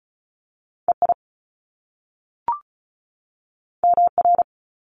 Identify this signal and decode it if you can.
Morse code, heard as a keyed tone, about 36 wpm, 700 Hz. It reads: EI MR
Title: Top Two Letter Combinations in Words 35wpm